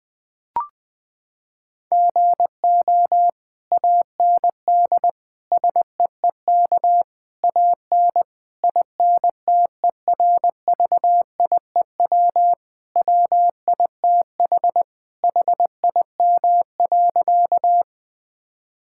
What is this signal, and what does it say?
Morse code, heard as a keyed tone, about 20 wpm, 700 Hz. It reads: GO AND SEEK AN INTERVIEW WITH HIM.
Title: Sentences from Top 1600 Words 20wpm